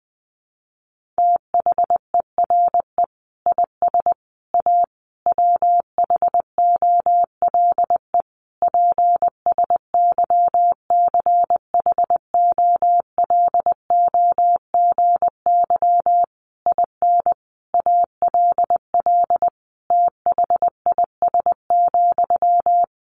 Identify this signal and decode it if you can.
Morse code, heard as a keyed tone, about 19 wpm, 700 Hz. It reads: THERE IS A WHOLE PSYCHOLOGY IN ALL THIS,
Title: Sentences from Top 1400 Words 20wpm